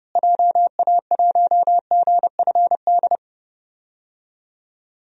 Morse code, heard as a keyed tone, about 30 wpm, 700 Hz. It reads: JA1GFD
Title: DXCC-100 Call Signs - 6 Characters - Repeated 3x - 30wpm